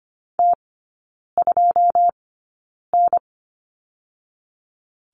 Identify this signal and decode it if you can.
Morse code, heard as a keyed tone, about 25 wpm, spaced as for 9 wpm, 690 Hz. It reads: T2N
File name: Lesson 14e - Number 2 - Three Characters 25wpm